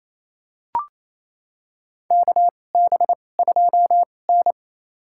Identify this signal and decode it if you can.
Morse code, heard as a keyed tone, about 28 wpm, spaced as for 20 wpm, 700 Hz. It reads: KB2N
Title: USA Call Signs - 4 Characters - 2x Character Spacing - 28wpm